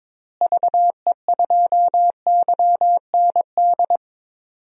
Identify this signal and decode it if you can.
Morse code, heard as a keyed tone, about 22 wpm, 700 Hz. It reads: VE2YND
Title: DXCC-100 Call Signs - 6 Characters - Repeated 3x - 22wpm